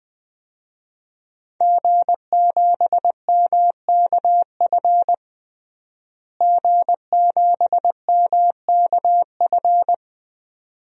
Morse code, heard as a keyed tone, about 21 wpm, 700 Hz. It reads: G7MKF G7MKF
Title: DXCC-100 Call Signs - 5 Characters - Repeated 2x - 20wpm